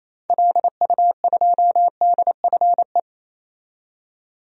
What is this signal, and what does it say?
Morse code, heard as a keyed tone, about 28 wpm, 700 Hz. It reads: LU2DFE
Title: DXCC-100 Call Signs - 6 Characters - 28wpm